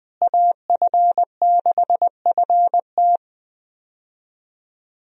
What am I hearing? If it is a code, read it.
Morse code, heard as a keyed tone, about 20 wpm, 700 Hz. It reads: AF6FT
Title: USA Call Signs - 5 Characters - Repeated 3x - 20wpm